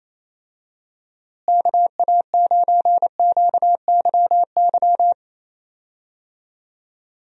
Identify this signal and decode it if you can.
Morse code, heard as a keyed tone, about 28 wpm, 700 Hz. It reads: KA9QYY
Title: USA Call Signs - 6 Characters - Repeated 2x - 28wpm